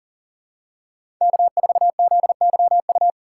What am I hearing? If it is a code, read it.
Morse code, heard as a keyed tone, about 40 wpm, 700 Hz. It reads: K4ZYU
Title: USA Call Signs - 4x Character Spacing - 40wpm